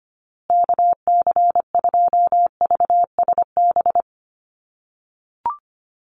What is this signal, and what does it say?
Morse code, heard as a keyed tone, about 25 wpm, 700 Hz. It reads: KC2VSB
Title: USA Call Signs Repeated 3x 25wpm